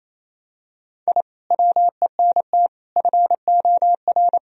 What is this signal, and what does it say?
Morse code, heard as a keyed tone, about 28 wpm, 700 Hz. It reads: I WENT FOR
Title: Sentences from Top 1000 Words - Review Part 1 - 28wpm